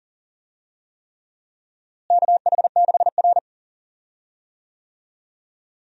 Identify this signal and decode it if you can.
Morse code, heard as a keyed tone, about 37 wpm, 700 Hz. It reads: KH6R
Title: USA Call Signs - 4 Characters - Repeated 3x - 40wpm